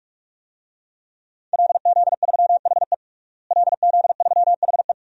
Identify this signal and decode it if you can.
Morse code, heard as a keyed tone, about 46 wpm, 700 Hz. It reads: LZ3HE LZ3HE
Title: DXCC-100 Call Signs - 5 Characters - Repeated 2x - 45wpm